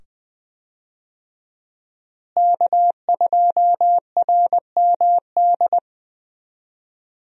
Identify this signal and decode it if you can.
Morse code, heard as a keyed tone, about 20 wpm, 700 Hz. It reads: K2RMD
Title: USA Call Signs - 5 Characters - Repeated 2x - 20wpm